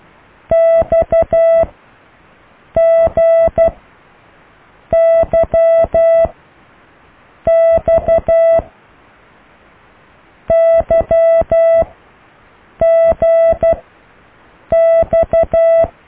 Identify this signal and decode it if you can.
Morse code, heard as a keyed tone, about 12 wpm, 660 Hz. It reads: X G Y X Y G X